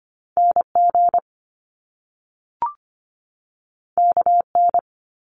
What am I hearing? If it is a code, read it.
Morse code, heard as a keyed tone, about 25 wpm, 700 Hz. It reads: NG KN
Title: Top Two Letter Combinations in Words 25wpm